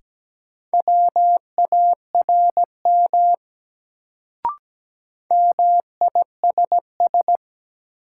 Morse code, heard as a keyed tone, about 17 wpm, 700 Hz. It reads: WARM MISS
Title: Sentences from Top 500 Words - New Word Drill - 17wpm